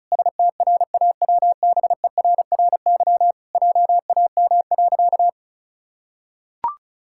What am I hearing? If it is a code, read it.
Morse code, heard as a keyed tone, about 34 wpm, 700 Hz. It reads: STRAWBERRY JAM.
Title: Pangram Challenge v2 - 35wpm